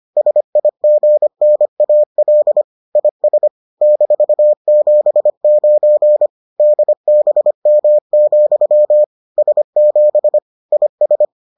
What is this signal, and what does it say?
Morse code, heard as a keyed tone, about 25 wpm, 600 Hz. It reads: SIGNAL IS -79 DBM, S7 IS